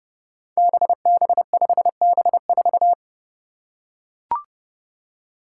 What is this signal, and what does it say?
Morse code, heard as a keyed tone, about 30 wpm, 700 Hz. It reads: BB5B4
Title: Five Challenging Letter-Number Combinations 30wpm